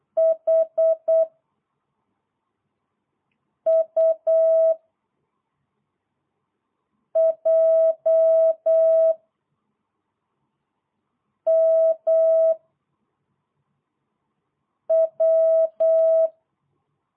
Morse code, heard as a keyed tone, about 8 wpm, 660 Hz. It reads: H U J M W